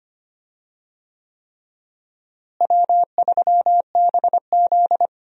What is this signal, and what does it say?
Morse code, heard as a keyed tone, about 25 wpm, 700 Hz. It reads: W3BZ
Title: USA Call Signs - 4 Characters - Repeated 3x - 25wpm